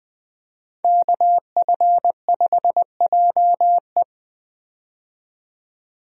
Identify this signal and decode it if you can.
Morse code, heard as a keyed tone, about 21 wpm, 700 Hz. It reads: KF5JE
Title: USA Call Signs - 5 Characters - 20wpm